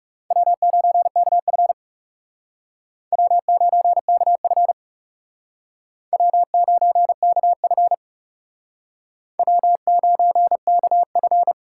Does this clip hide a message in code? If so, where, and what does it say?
Yes: Morse code from 0.3 to 11.5 s, heard as a keyed tone, about 38 wpm, 700 Hz. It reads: W9KF W9KF W9KF W9KF